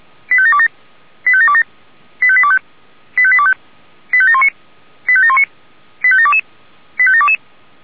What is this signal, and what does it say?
Five-tone selective call (ZVEI), heard as quick runs of five tones, tones of about 70 ms. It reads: 85727 85727 85825 85825 85719 85719 85722 85722